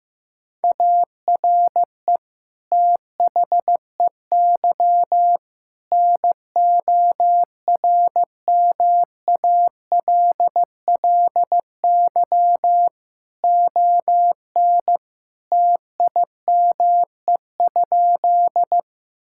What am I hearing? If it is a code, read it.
Morse code, heard as a keyed tone, about 15 wpm, 700 Hz. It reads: ARE THEY NORMALLY ON TIME?
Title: Sentences from Top 1000 Words - Review Part 6 - 15wpm